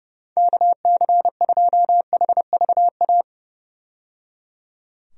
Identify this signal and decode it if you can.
Morse code, heard as a keyed tone, about 31 wpm, 700 Hz. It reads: KC2HVA
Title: USA Call Signs - 6 Characters - 30wpm